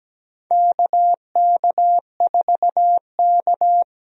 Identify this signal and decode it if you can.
Morse code, heard as a keyed tone, about 17 wpm, 700 Hz. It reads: KK4K
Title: USA Call Signs - 4 Characters - 17wpm